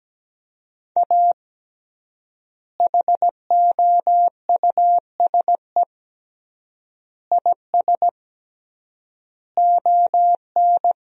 Morse code, heard as a keyed tone, about 17 wpm, 700 Hz. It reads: A HOUSE IS ON
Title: 3x Word-Spacing - Sentences from Top 400 Words - 17wpm